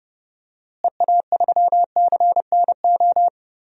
Morse code, heard as a keyed tone, about 30 wpm, 700 Hz. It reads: EA3CNO
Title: DXCC-100 Call Signs - 6 Characters - Repeated 2x - 30wpm